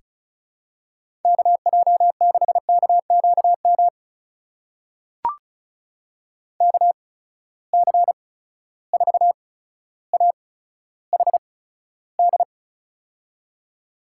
Morse code, heard as a keyed tone, about 36 wpm, spaced as for 15 wpm, 700 Hz. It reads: KJ6KQM KC4AHD
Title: USA Call Signs - 6 Characters - 8x Character Spacing - 35wpm